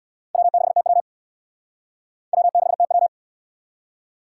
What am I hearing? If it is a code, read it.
Morse code, heard as a keyed tone, about 77 wpm, 700 Hz. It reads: W6TW W6TW